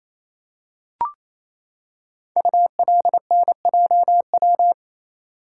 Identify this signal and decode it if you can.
Morse code, heard as a keyed tone, about 27 wpm, 700 Hz. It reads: ULNJW
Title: Code Groups - 5 Letters - 28wpm